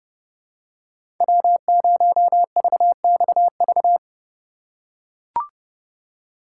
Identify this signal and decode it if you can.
Morse code, heard as a keyed tone, about 30 wpm, 700 Hz. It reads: W0VXV
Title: USA Call Signs - 5 Characters - 2x Character Spacing - 30wpm